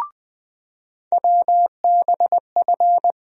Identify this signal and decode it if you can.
Morse code, heard as a keyed tone, about 21 wpm, 700 Hz. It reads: WBF
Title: USA Call Sign Postfixes - 3 Letters 20wpm